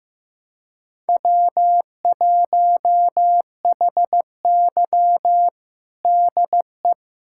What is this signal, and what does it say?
Morse code, heard as a keyed tone, about 15 wpm, 700 Hz. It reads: W1HY DE
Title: USA Callsign DE Callsign - 4 Characters - 15wpm